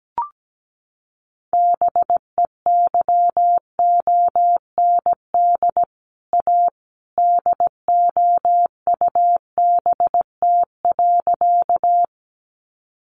Morse code, heard as a keyed tone, about 17 wpm, 700 Hz. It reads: BEYOND A DOUBT.
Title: Sentences from Top 1500 Words 17wpm